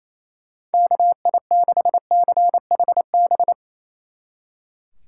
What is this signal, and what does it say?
Morse code, heard as a keyed tone, about 36 wpm, 700 Hz. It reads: KI6CHB